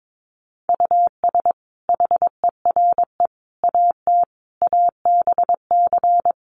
Morse code, heard as a keyed tone, about 22 wpm, 700 Hz. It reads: US HERE AT ABC
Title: Sets of 5 Words 22wpm